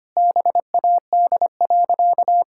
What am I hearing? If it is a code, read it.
Morse code, heard as a keyed tone, about 25 wpm, 700 Hz. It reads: BAD.
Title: Sentences from Top 1000 Words - Review Part 6 - 25wpm